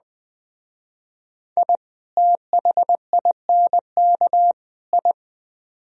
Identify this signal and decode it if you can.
Morse code, heard as a keyed tone, about 19 wpm, 700 Hz. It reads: I THINK I